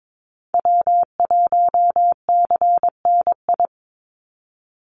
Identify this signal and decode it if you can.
Morse code, heard as a keyed tone, about 22 wpm, 700 Hz. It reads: W1CNI
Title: USA Call Signs 22wpm